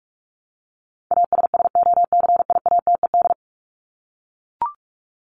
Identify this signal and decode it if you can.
Morse code, heard as a keyed tone, about 44 wpm, 700 Hz. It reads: ASSOCIATED
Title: ED Suffix - Top 100 Words with Base Words 45wpm